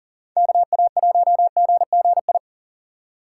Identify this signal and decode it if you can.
Morse code, heard as a keyed tone, about 52 wpm, 700 Hz. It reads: KA1GGI